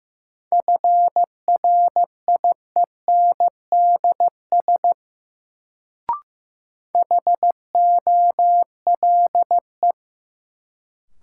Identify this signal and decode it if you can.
Morse code, heard as a keyed tone, about 15 wpm, 700 Hz. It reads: FRIENDS HOLE
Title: Sentences from Top 1600 Words - New Word Drill Alt - 15wpm